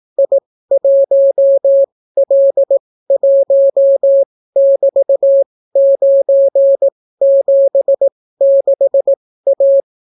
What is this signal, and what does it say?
Morse code, heard as a keyed tone, about 18 wpm, spaced as for 14 wpm, 550 Hz. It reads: I1L1<BT>976A